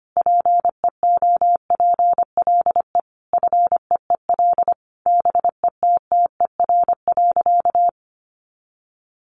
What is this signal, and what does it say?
Morse code, heard as a keyed tone, about 25 wpm, 700 Hz. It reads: PEOPLE FEEL BETTER.
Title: Easy Phrases 25wpm